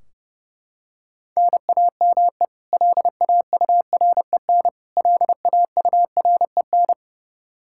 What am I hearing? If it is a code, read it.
Morse code, heard as a keyed tone, about 30 wpm, 700 Hz. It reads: NAME LAUREN LAUREN
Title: Practice QSO - Exchange 1 - Beginning - 30wpm